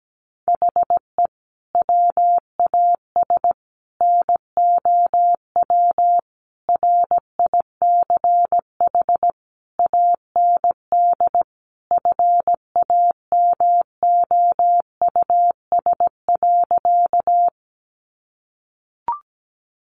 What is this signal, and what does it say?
Morse code, heard as a keyed tone, about 17 wpm, 700 Hz. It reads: HE WAS NOW RICH AND FAMOUS.
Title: Sentences from Top 1500 Words 17wpm